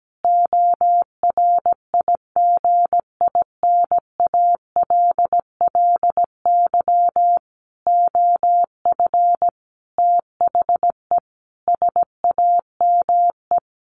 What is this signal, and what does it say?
Morse code, heard as a keyed tone, about 17 wpm, 700 Hz. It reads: ORIGINALLY OF THE SAME
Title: Sentences from Top 1500 Words 17wpm